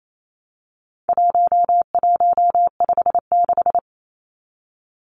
Morse code, heard as a keyed tone, about 28 wpm, 700 Hz. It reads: 1156
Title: Numbers - 4-Digit 28wpm